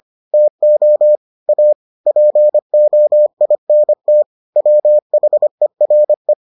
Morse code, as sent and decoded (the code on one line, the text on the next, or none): - --- / .- / .--. --- .. -. - / .-- .... .-. .
TO A POINT WHERE